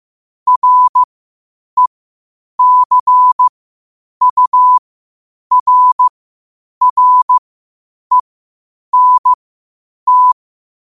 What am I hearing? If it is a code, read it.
Morse code, heard as a keyed tone, about 15 wpm, spaced as for 8 wpm, 1000 Hz. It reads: RECURRENT
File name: qotdfiles / 8wpm